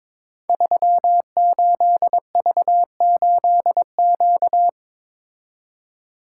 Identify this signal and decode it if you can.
Morse code, heard as a keyed tone, about 23 wpm, 700 Hz. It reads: 38V8Q
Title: Code Groups - 5 Letter:Numbers - 22wpm